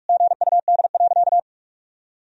Morse code, heard as a keyed tone, about 42 wpm, 700 Hz. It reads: GUD.
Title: Sentences from Top 200 Words and Abbreviations 45wpm